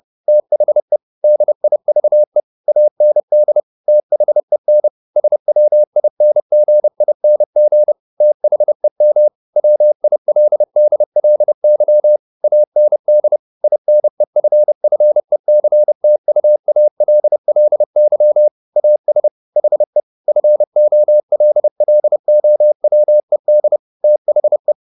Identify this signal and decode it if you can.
Morse code, heard as a keyed tone, about 30 wpm, 600 Hz. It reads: THE DIVE AND THEN SWINGING THEM WILDLY AND INEFFECTUALLY AS HE FOLLOWED THE
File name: the old man and the sea / the old man and the sea 1 30wpm